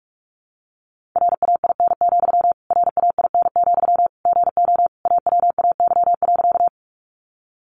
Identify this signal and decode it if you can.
Morse code, heard as a keyed tone, about 44 wpm, 700 Hz. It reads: RAIN, RAIN, GO AWAY.
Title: Sentences from Top 600 Words 45wpm